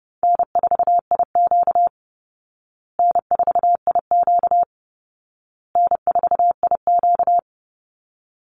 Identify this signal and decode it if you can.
Morse code, heard as a keyed tone, about 30 wpm, 700 Hz. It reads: N4IQ N4IQ N4IQ
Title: Call Signs Repeated 2x - 30wpm